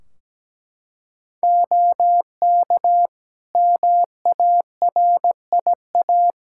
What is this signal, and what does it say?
Morse code, heard as a keyed tone, about 17 wpm, 700 Hz. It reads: OK MARIA
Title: Practice QSO - Exchange 2 - Info - 17wpm